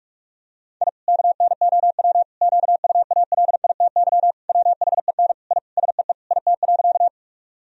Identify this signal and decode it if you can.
Morse code, heard as a keyed tone, about 46 wpm, 700 Hz. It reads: I KNOW QUALITY WHEN I SEE IT.
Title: Sentences from Top 1000 Words - Review Part 4 - 45wpm